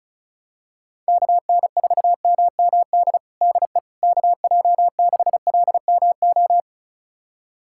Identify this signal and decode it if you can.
Morse code, heard as a keyed tone, about 36 wpm, 700 Hz. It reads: KN4MMD DE KJ6LMO
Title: USA Callsign DE Callsign - 6 Characters - 35wpm